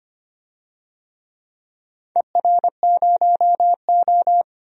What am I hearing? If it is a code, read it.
Morse code, heard as a keyed tone, about 25 wpm, 700 Hz. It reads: ER0O
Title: DXCC-100 Call Signs - 4 Characters - 25wpm